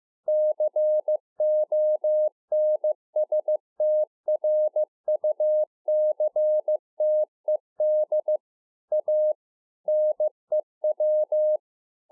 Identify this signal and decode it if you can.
Morse code, heard as a keyed tone, about 15 wpm, 610 Hz. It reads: CONSTRUCTED A NEW